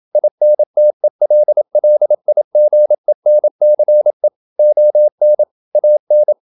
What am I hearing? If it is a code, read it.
Morse code, heard as a keyed tone, about 27 wpm, 600 Hz. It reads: INTELLIGENCE ON AN
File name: Never 0027wpm